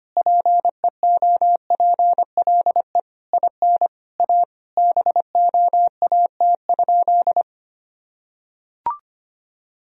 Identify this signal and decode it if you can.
Morse code, heard as a keyed tone, about 25 wpm, 700 Hz. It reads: PEOPLE IN A BOAT?
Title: Sentences from Top 1000 Words - Review Part 3 - 25wpm